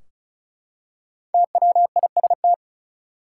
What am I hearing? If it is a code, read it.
Morse code, heard as a keyed tone, about 36 wpm, 700 Hz. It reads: TWIST
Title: Five Letter Words - 35wpm